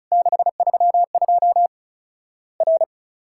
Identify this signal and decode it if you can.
Morse code, heard as a keyed tone, about 34 wpm, 700 Hz. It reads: 632 R
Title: Worked All Europe DX Contest - QTC Exchange - 35wpm